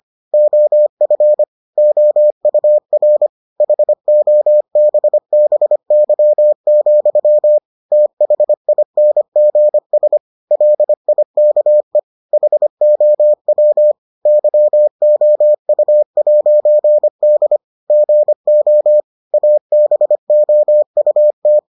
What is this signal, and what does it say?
Morse code, heard as a keyed tone, about 25 wpm, 600 Hz. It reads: OF OUR HOBBY, THINGS LIKE HOW YOU'D GO ABOUT